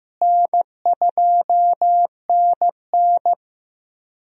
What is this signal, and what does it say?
Morse code, heard as a keyed tone, about 16 wpm, 700 Hz. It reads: N2NN
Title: USA Call Signs - 4 Characters - Repeated 3x - 15wpm